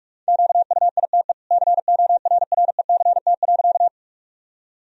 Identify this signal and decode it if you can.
Morse code, heard as a keyed tone, about 46 wpm, 700 Hz. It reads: QUITE CORRECT.
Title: Sentences from Top 1000 Words - Review Part 2 - 45wpm